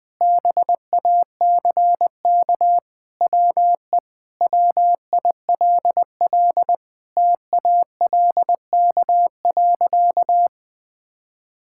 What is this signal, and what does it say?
Morse code, heard as a keyed tone, about 20 wpm, 700 Hz. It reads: BACK WE WILL TALK.
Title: Sentences with 4-Word Phrases - Spoken Beginning - 20wpm